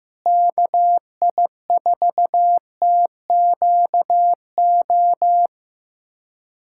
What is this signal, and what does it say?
Morse code, heard as a keyed tone, about 15 wpm, 700 Hz. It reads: KI4TQO
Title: USA Callsign DE Callsign - 6 Characters - 15wpm